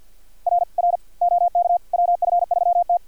Morse code, heard as a keyed tone, about 50 wpm, 700 Hz, with radio noise. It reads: R R OK WF3T
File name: archived / CRSnet